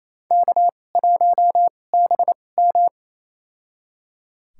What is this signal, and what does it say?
Morse code, heard as a keyed tone, about 27 wpm, spaced as for 20 wpm, 700 Hz. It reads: K1BM
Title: USA Call Signs - 4 Characters - 2x Character Spacing - 28wpm